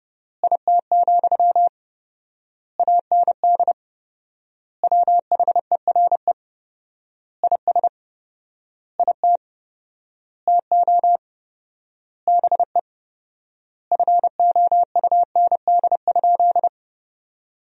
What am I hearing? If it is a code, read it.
Morse code, heard as a keyed tone, about 30 wpm, 700 Hz. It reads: IT, AND WHERE IS IT TO BE FOUND?
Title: 4x Word-Spacing - Sentences from Top 300 Words - 30wpm